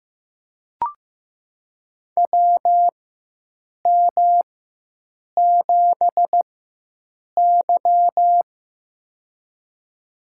Morse code, heard as a keyed tone, about 15 wpm, spaced as for 7 wpm, 700 Hz. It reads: WM7Y
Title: USA Call Signs - 4x Character Spacing - 15wpm